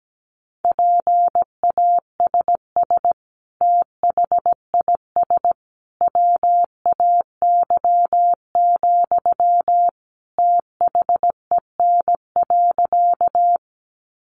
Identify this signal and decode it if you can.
Morse code, heard as a keyed tone, about 17 wpm, 700 Hz. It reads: PASS THIS WAY, THEN.
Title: Sentences from Top 400 Words 17wpm